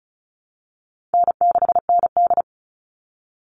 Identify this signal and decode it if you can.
Morse code, heard as a keyed tone, about 34 wpm, 700 Hz. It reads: N6ND
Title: All Cut Numbers - 4-Digit 35wpm